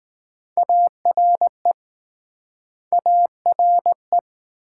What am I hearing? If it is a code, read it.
Morse code, heard as a keyed tone, about 19 wpm, 700 Hz. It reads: ARE ARE